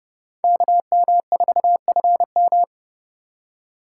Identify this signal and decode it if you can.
Morse code, heard as a keyed tone, about 36 wpm, 700 Hz. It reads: KM4FM
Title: Speed Racing - USA Call Signs 30wpm